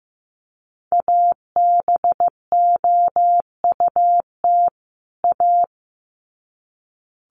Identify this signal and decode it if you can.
Morse code, heard as a keyed tone, about 15 wpm, 700 Hz. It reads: ABOUT A
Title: Sets of 2 Words 15wpm